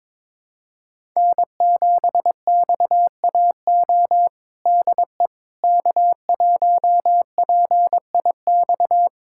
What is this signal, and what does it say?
Morse code, heard as a keyed tone, about 22 wpm, 700 Hz. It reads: N7XAO DE K1PIX
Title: USA Callsign DE Callsign - 5 Characters - 22wpm